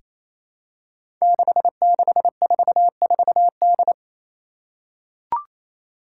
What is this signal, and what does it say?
Morse code, heard as a keyed tone, about 28 wpm, 700 Hz. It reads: 6644D
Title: Five Challenging Letter-Number Combinations 28wpm